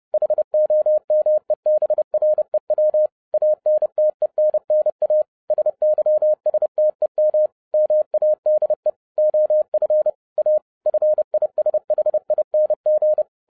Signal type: Morse code